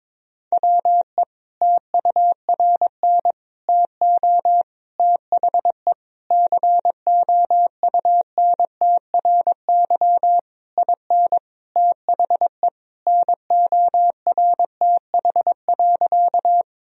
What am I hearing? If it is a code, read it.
Morse code, heard as a keyed tone, about 22 wpm, 700 Hz. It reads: WE TURN TO THE COUNTRY IN THE NORTH.
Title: Sentences from Top 1000 Words - Review Part 1 - 22wpm